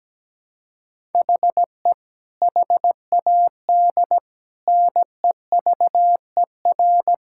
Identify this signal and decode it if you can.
Morse code, heard as a keyed tone, about 17 wpm, 700 Hz. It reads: HE HAD NEVER
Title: Sentences from Top 1000 Words - Review Part 5 - 17wpm